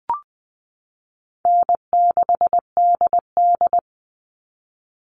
Morse code, heard as a keyed tone, about 20 wpm, 700 Hz. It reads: N6DD
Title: All Cut Numbers - 4-Digit 20wpm